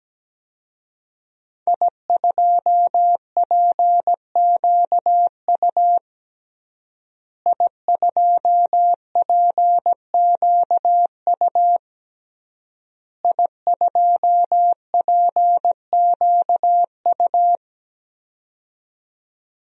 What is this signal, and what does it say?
Morse code, heard as a keyed tone, about 17 wpm, 700 Hz. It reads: I2PQU I2PQU I2PQU
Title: DXCC-100 Call Signs - 5 Characters - Repeated 3x - 17wpm